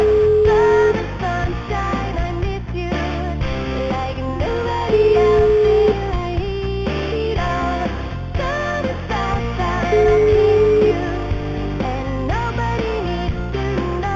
Ringback tone (Europe, 425 Hz), with noise.